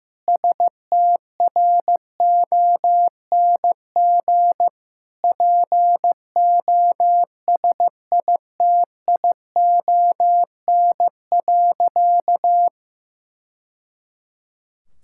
Morse code, heard as a keyed tone, about 15 wpm, 700 Hz. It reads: STRONG POSITION.
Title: Sentences from Top 1000 Words - Review Part 5 - 15wpm